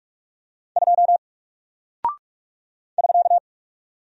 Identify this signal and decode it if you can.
Morse code, heard as a keyed tone, about 46 wpm, 700 Hz. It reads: 2 <SK>
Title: Single Character (Letter:Number:Punctuation:Prosign) Alt 45wpm